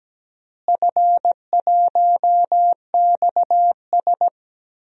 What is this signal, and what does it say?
Morse code, heard as a keyed tone, about 17 wpm, 700 Hz. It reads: F1XS
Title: DXCC-100 Call Signs - 4 Characters - Repeated 2x - 17wpm